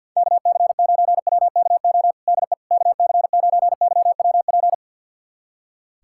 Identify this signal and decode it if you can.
Morse code, heard as a keyed tone, about 50 wpm, 700 Hz. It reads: KC9WKO DE KC9YWP
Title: USA Callsign DE Callsign 50wpm